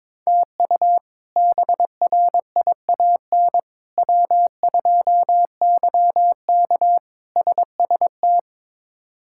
Morse code, heard as a keyed tone, about 22 wpm, 700 Hz. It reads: TU BRIAN W2YK SST
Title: K1USN SST QSO Practice - 22wpm